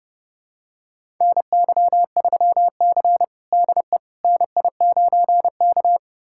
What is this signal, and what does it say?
Morse code, heard as a keyed tone, about 32 wpm, 700 Hz. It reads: NY3C DE NI9K
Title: USA Callsign DE Callsign - 4 Characters - 30wpm